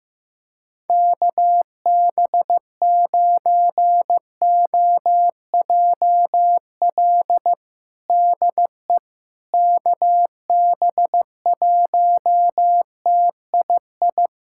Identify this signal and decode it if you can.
Morse code, heard as a keyed tone, about 16 wpm, 700 Hz. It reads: KB9OJL DE KB1TII
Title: USA Callsign DE Callsign 15wpm